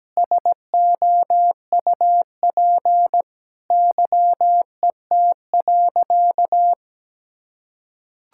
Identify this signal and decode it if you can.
Morse code, heard as a keyed tone, about 17 wpm, 700 Hz. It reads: SOUP YET.
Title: Sentences from Top 1000 Words - Review Part 6 - 17wpm